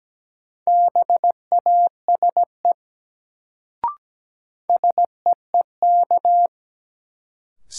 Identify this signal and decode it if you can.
Morse code, heard as a keyed tone, about 17 wpm, 700 Hz. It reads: BASE SEEK
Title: Sentences from Top 1600 Words - New Word Drill Alt - 17wpm